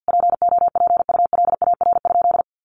Morse code, heard as a keyed tone, about 50 wpm, 700 Hz. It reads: POPULAR?
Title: Sentences from Top 1200 Words 50wpm